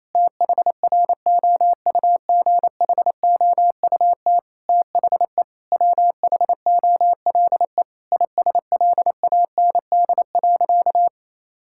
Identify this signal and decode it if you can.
Morse code, heard as a keyed tone, about 28 wpm, 700 Hz. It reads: THROUGHOUT THE WHOLE ISLAND.
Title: Sentences from Top 1000 Words - Review Part 5 - 28wpm